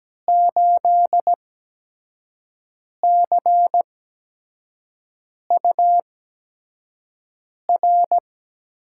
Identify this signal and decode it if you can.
Morse code, heard as a keyed tone, about 17 wpm, spaced as for 5 wpm, 700 Hz. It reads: 8CUR
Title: USA Call Signs - 6 Characters - 8x Character Spacing - 17wpm